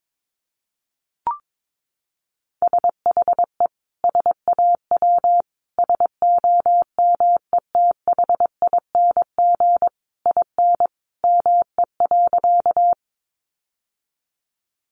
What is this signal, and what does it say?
Morse code, heard as a keyed tone, about 22 wpm, 700 Hz. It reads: SHE SAW SOMETHING IN ME.
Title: Sentences from Top 600 Words 22wpm